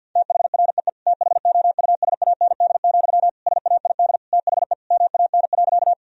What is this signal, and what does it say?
Morse code, heard as a keyed tone, about 50 wpm, 700 Hz. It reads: THREE THOUSAND, SAID THE MAN.
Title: Sentences from Top 1000 Words - Review Part 3 - 50wpm